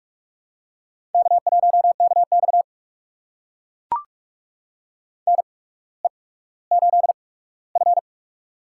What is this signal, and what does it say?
Morse code, heard as a keyed tone, about 42 wpm, spaced as for 19 wpm, 700 Hz. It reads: K1KX NE8F
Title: USA Call Signs - 4 Characters - 8x Character Spacing - 45wpm